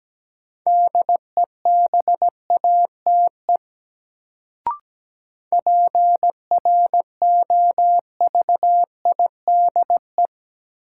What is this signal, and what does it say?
Morse code, heard as a keyed tone, about 17 wpm, 700 Hz. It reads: DEBATE PROVIDE
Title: Sentences from Top 1100 Words - New Word Drill Alt - 17wpm